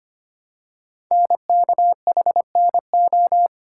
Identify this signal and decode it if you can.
Morse code, heard as a keyed tone, about 25 wpm, 700 Hz. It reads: NKHNO
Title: Code Groups - 5 Letters - 25wpm